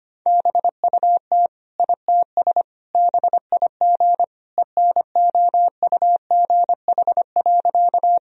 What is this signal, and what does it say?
Morse code, heard as a keyed tone, about 25 wpm, 700 Hz. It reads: BUT ITS BIG ENOUGH.
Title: Sentences from Top 1000 Words - Review Part 1 - 25wpm